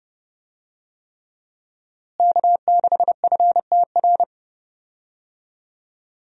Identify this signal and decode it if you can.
Morse code, heard as a keyed tone, about 30 wpm, 700 Hz. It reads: K6FTR
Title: USA Call Signs - 5 Characters - Repeated 3x - 30wpm